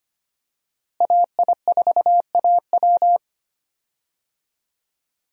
Morse code, heard as a keyed tone, about 25 wpm, 700 Hz. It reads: AI4AW
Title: USA Call Signs - 5 Characters - Repeated 3x - 25wpm